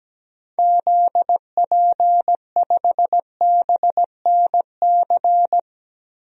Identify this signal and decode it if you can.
Morse code, heard as a keyed tone, about 17 wpm, 700 Hz. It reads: ZP5BNC
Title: DXCC-100 Call Signs - 6 Characters - 17wpm